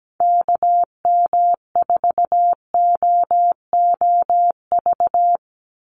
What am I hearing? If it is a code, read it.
Morse code, heard as a keyed tone, about 17 wpm, 700 Hz. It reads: KM4OOV
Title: USA Call Signs 17wpm